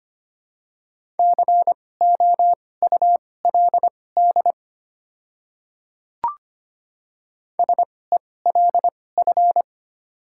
Morse code, heard as a keyed tone, about 25 wpm, spaced as for 18 wpm, 700 Hz. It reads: COULD SELF
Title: 2x Character Spacing - Top 300 Words - 25wpm